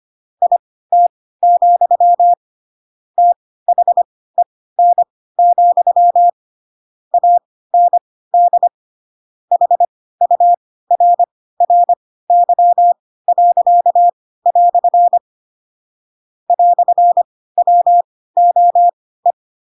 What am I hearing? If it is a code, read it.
Morse code, heard as a keyed tone, about 25 wpm, spaced as for 16 wpm, 700 Hz. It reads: IT, THEN, AND HURRY." "WOE
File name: Pinocchio / Pinocchio ch4 10 25 ik1hge